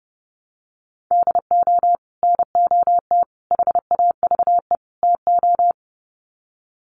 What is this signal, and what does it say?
Morse code, heard as a keyed tone, about 30 wpm, 700 Hz. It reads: DO NOT HAVE TO
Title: Sets of 4 Words 30wpm